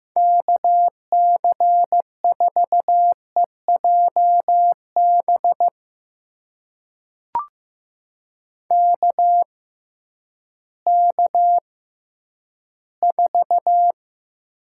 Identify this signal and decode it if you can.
Morse code, heard as a keyed tone, about 15 wpm, spaced as for 10 wpm, 700 Hz. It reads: KC4EJB KK4
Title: USA Call Signs - 6 Characters - 6x Character Spacing - 15wpm